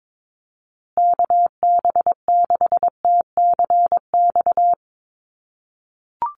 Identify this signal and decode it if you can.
Morse code, heard as a keyed tone, about 22 wpm, 700 Hz. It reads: KB6TCX
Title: USA Call Signs Repeated 2x 22wpm